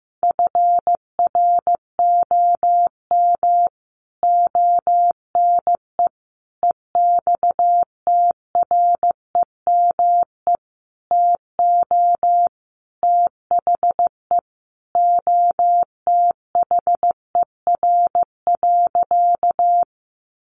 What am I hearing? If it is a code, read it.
Morse code, heard as a keyed tone, about 15 wpm, 700 Hz. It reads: FROM ONE EXTREME TO THE OTHER.
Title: Sentences from Top 1300 Words 15wpm